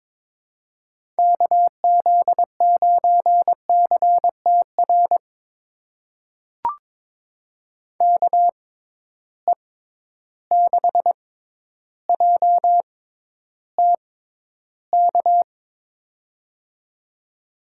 Morse code, heard as a keyed tone, about 22 wpm, spaced as for 11 wpm, 700 Hz. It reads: KZ9CTR KE6JTK
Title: USA Call Signs - 6x Character Spacing - 22wpm